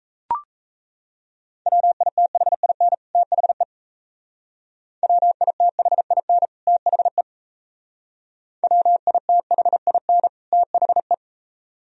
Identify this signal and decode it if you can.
Morse code, heard as a keyed tone, about 37 wpm, 690 Hz. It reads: WITHIN THE WITHIN THE WITHIN THE